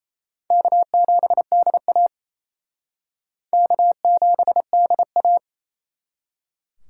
Morse code, heard as a keyed tone, about 34 wpm, 700 Hz. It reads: K7DA K7DA
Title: Speed Racing - USA 1x2 Call Signs 28wpm